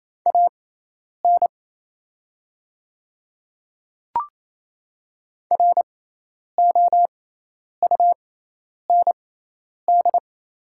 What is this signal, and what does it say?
Morse code, heard as a keyed tone, about 28 wpm, spaced as for 10 wpm, 700 Hz. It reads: AN ROUND
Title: 6x Character Spacing - Top 300 Words - 28wpm